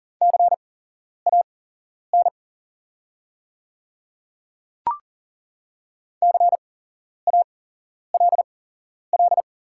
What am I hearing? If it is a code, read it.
Morse code, heard as a keyed tone, about 40 wpm, spaced as for 11 wpm, 700 Hz. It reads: CAN CALL
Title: 8x Character Spacing - Top 100 Words - 40wpm